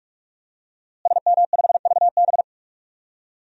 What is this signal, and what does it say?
Morse code, heard as a keyed tone, about 42 wpm, 700 Hz. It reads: SM5VB
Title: DXCC-100 Call Signs - 5 Characters - Repeated 3x - 45wpm